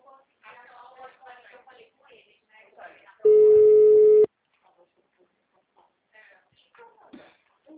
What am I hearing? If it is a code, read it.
Ringback tone (Europe, 425 Hz), with speech.